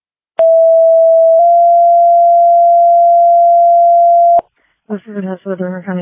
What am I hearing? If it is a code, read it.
Two-tone page, heard as a one-second tone, then a three-second tone, about 660 then 690 Hz, with speech. It is heard from 0.4 to 4.4 s.